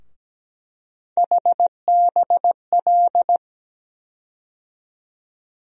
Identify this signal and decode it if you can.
Morse code, heard as a keyed tone, about 17 wpm, 700 Hz. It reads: HBL
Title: USA Call Sign Postfixes - 3 Letters 17wpm